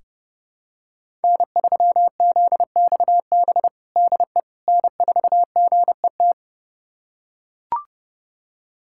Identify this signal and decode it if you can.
Morse code, heard as a keyed tone, about 30 wpm, 700 Hz. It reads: N3ZXB DE N4GET
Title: USA Callsign DE Callsign - 5 Characters - 30wpm